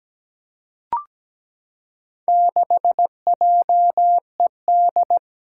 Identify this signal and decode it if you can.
Morse code, heard as a keyed tone, about 17 wpm, 700 Hz. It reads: 6JED
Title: Four Challenging Letter-Number Combinations 17wpm